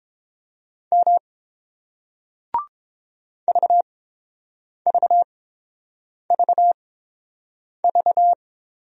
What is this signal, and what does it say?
Morse code, heard as a keyed tone, about 29 wpm, spaced as for 7 wpm, 700 Hz. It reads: M VVVV